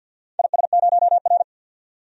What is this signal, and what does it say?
Morse code, heard as a keyed tone, about 46 wpm, 700 Hz. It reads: IS0R
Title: DXCC-100 Call Signs - 4 Characters - 50wpm